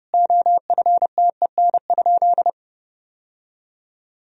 Morse code, heard as a keyed tone, about 30 wpm, 700 Hz. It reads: OFTEN?
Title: Sentences from Top 1000 Words - Review Part 4 - 30wpm